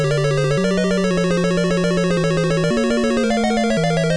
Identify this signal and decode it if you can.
There is no code.